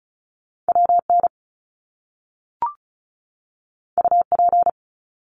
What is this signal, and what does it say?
Morse code, heard as a keyed tone, about 36 wpm, 700 Hz. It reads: WN UP
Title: Top Two Letter Combinations in Words 35wpm